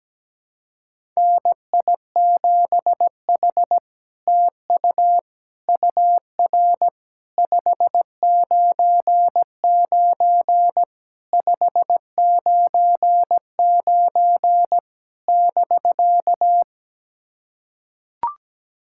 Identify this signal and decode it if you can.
Morse code, heard as a keyed tone, about 17 wpm, 700 Hz. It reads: NI7H TU UR 599 599 <BK>
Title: SOTA QSO Practice 17wpm